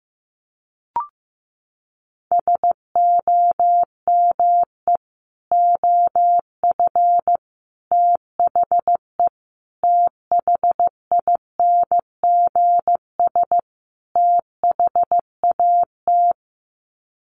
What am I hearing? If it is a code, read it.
Morse code, heard as a keyed tone, about 15 wpm, 700 Hz. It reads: SOME OF THE THINGS THAT
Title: Sets of 5 Words 15wpm